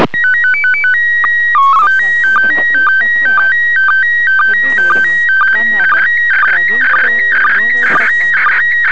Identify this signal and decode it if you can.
SSTV (slow-scan television), PD 120, with speech.